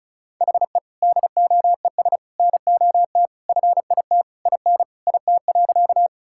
Morse code, heard as a keyed tone, about 36 wpm, 700 Hz. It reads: HE DOES NOT FIT IN IT.
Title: Sentences from Top 1000 Words - Review Part 5 - 35wpm